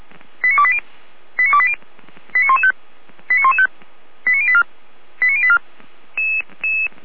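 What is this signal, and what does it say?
Five-tone selective call (ZVEI), heard as quick runs of five tones, tones of about 70 ms. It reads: 89280 89280 89115 89115 89084 89084
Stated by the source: Five-tone ZVEI selective call received by SDR.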